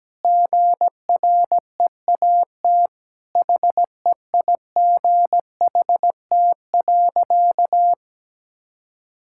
Morse code, heard as a keyed tone, about 17 wpm, 700 Hz. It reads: GREAT HEIGHT.
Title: Sentences from Top 1600 Words 17wpm